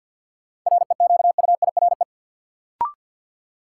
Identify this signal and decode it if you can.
Morse code, heard as a keyed tone, about 53 wpm, 700 Hz. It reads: REQUIRE
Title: Sentences from Top 900 Words - New Word Drill - 50wpm